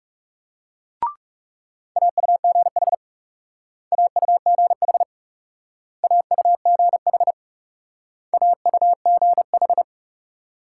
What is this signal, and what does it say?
Morse code, heard as a keyed tone, about 38 wpm, 700 Hz. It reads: AUGH AUGH AUGH AUGH
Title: Speed-Racing Multi-letter Phonemes - 30wpm